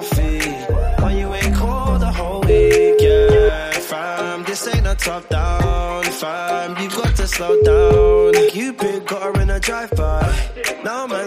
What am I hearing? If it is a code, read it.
Ringback tone (Europe, 425 Hz), with noise.